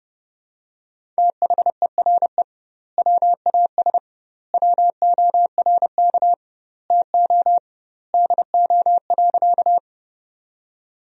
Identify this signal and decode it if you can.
Morse code, heard as a keyed tone, about 29 wpm, 700 Hz. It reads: THERE WAS WORK TO DO.
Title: 2x Word-Spacing - Sentences from Top 200 Words - 30wpm